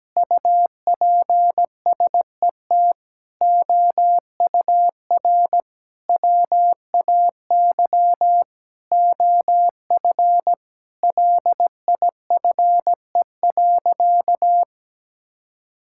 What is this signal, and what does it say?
Morse code, heard as a keyed tone, about 17 wpm, 700 Hz. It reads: UPSET OUR WAY OF LIFE.
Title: Sentences from Top 1000 Words - Review Part 6 - 17wpm